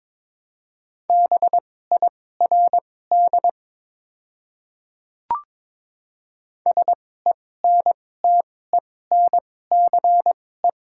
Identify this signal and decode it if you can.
Morse code, heard as a keyed tone, about 22 wpm, spaced as for 16 wpm, 700 Hz. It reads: BIRD SENTENCE
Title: 2x Character Spacing - Top 500 Words - 22wpm